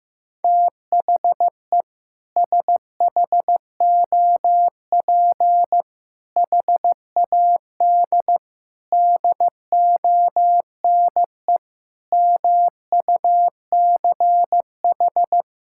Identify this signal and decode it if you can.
Morse code, heard as a keyed tone, about 15 wpm, 700 Hz. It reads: THE SHOP HAD DONE MUCH
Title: Sentences from Top 1600 Words 15wpm